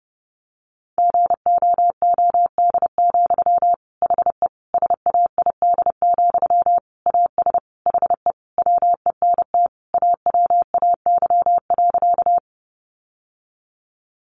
Morse code, heard as a keyed tone, about 30 wpm, 700 Hz. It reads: GOOD, HE SAID, AS HE WENT AWAY.
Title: Sentences from Top 600 Words 30wpm